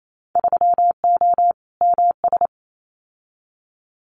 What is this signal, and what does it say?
Morse code, heard as a keyed tone, about 28 wpm, 700 Hz. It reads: 3O MS
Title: Winter Field Day Exchanges Encore 28wpm